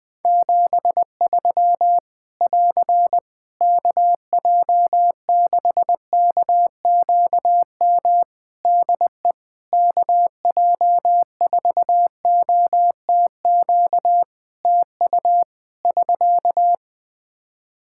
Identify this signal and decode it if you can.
Morse code, heard as a keyed tone, about 20 wpm, 700 Hz. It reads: 73 <AR> KJ6KQM DE KJ4OTQ TU <SK>
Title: Practice QSO - Exchange 4 - Ending - 20wpm